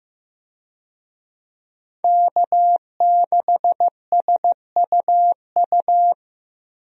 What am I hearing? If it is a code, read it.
Morse code, heard as a keyed tone, about 15 wpm, 700 Hz. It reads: K6SUU
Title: USA Call Signs - 5 Characters - Repeated 3x - 15wpm